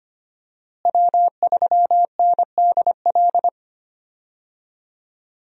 Morse code, heard as a keyed tone, about 25 wpm, 700 Hz. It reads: W3NDL
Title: USA Call Signs - 5 Characters - Repeated 3x - 25wpm